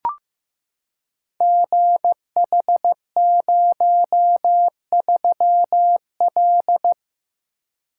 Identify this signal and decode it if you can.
Morse code, heard as a keyed tone, about 15 wpm, 700 Hz. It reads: GH03L
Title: Code Groups - 5 Letter:Numbers - 15wpm